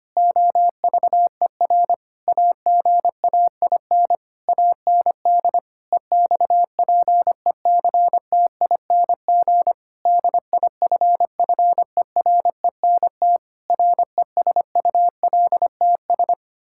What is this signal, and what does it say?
Morse code, heard as a keyed tone, about 25 wpm, 700 Hz. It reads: OVER AGAIN AND EXPECTING DIFFERENT RESULTS
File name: practice / 25wpm / Sentences with 9-Word Phrases - 25wpm